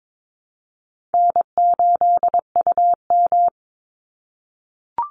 Morse code, heard as a keyed tone, about 22 wpm, 700 Hz. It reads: N8UM
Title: Call Signs - 22wpm